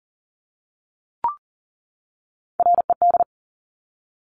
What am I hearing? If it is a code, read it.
Morse code, heard as a keyed tone, about 40 wpm, 700 Hz. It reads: RED
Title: Top Three Letter Combinations in Words 40wpm